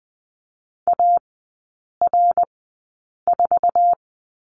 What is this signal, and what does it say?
Morse code, heard as a keyed tone, about 20 wpm, spaced as for 8 wpm, 700 Hz. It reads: AR4
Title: 053 Lesson 10e - Letter r - Three Characters 20wpm